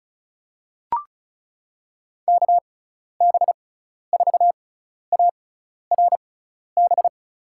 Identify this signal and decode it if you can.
Morse code, heard as a keyed tone, about 34 wpm, spaced as for 12 wpm, 700 Hz. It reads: KB4ARB
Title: USA Call Signs - 6x Character Spacing - 35wpm